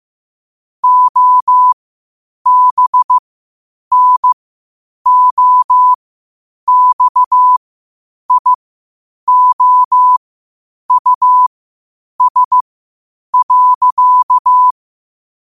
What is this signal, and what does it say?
Morse code, heard as a keyed tone, about 15 wpm, spaced as for 8 wpm, 1000 Hz. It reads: OBNOXIOUS.
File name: qotdfiles / 8wpm